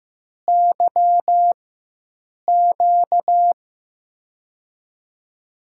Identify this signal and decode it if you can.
Morse code, heard as a keyed tone, about 15 wpm, spaced as for 7 wpm, 700 Hz. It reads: YQ
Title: USA Call Signs - 5 Characters - 4x Character Spacing - 15wpm